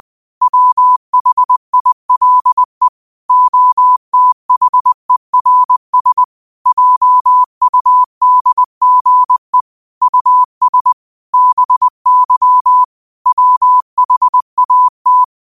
Morse code, heard as a keyed tone, about 20 wpm, 1000 Hz. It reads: WHILE OTHERS JUDGE US BY WHAT